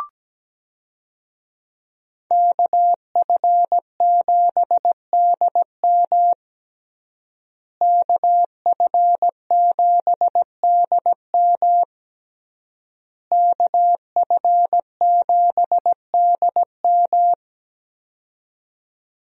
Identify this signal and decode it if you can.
Morse code, heard as a keyed tone, about 17 wpm, 700 Hz. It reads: KF7DM KF7DM KF7DM
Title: USA Call Signs - 5 Characters - Repeated 3x - 17wpm